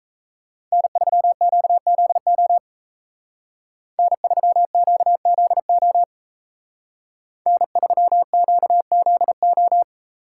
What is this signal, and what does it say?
Morse code, heard as a keyed tone, about 36 wpm, 700 Hz. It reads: N3QZO N3QZO N3QZO